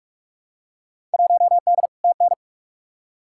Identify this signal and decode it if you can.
Morse code, heard as a keyed tone, about 42 wpm, 700 Hz. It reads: 1D TN
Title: Field Day Exchanges - 45wpm